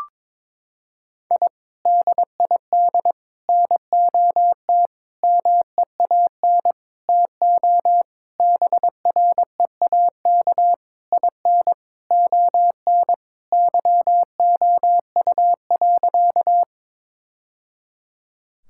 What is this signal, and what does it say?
Morse code, heard as a keyed tone, about 22 wpm, 700 Hz. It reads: I DID NOT MEAN TO BREAK IN ON YOU.
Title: Sentences from Top 1000 Words - Review Part 6 - 22wpm